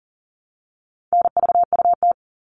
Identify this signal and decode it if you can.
Morse code, heard as a keyed tone, about 40 wpm, 700 Hz. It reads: NVUT
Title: All Cut Numbers - 4-Digit 40wpm